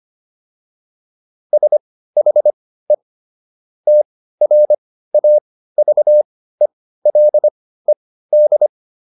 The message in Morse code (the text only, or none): SHE TRAVELED